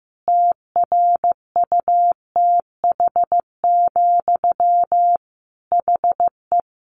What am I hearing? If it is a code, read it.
Morse code, heard as a keyed tone, about 15 wpm, 700 Hz. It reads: TRUTH, HE
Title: Sentences from Top 1500 Words 15wpm